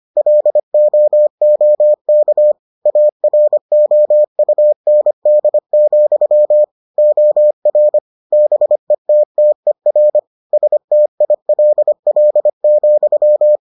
Morse code, heard as a keyed tone, about 25 wpm, 600 Hz. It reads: LOOK AROUND, OR BETTER STILL,